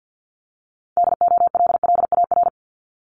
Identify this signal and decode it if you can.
Morse code, heard as a keyed tone, about 50 wpm, 700 Hz. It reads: DOLLAR
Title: Multi-letter Phonemes 50wpm